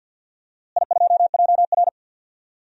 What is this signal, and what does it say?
Morse code, heard as a keyed tone, about 47 wpm, 700 Hz. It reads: I2JR
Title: DXCC-100 Call Signs - 4 Characters - Repeated 2x - 50wpm